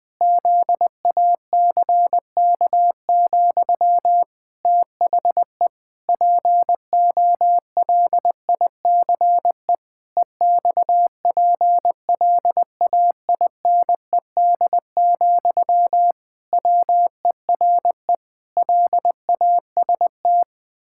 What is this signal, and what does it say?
Morse code, heard as a keyed tone, about 20 wpm, 700 Hz. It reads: ZACK, THE POLICE EXPLAINED, WERE LAST
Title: Pangram Challenge v2 - 20wpm